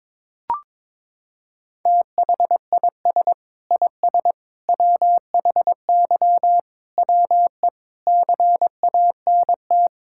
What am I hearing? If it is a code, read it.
Morse code, heard as a keyed tone, about 22 wpm, 700 Hz. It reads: THIS IS WHY WE CANT
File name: practice / 22wpm / Sentences with 8-Word Phrases - 22wpm